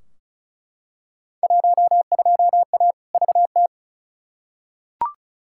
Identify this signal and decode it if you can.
Morse code, heard as a keyed tone, about 36 wpm, 700 Hz. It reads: 12A VT
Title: Field Day Exchanges Encore - 35wpm